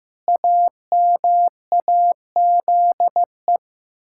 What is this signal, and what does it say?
Morse code, heard as a keyed tone, about 15 wpm, 700 Hz. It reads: AMAZE
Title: Pangram Challenge v2 - 15wpm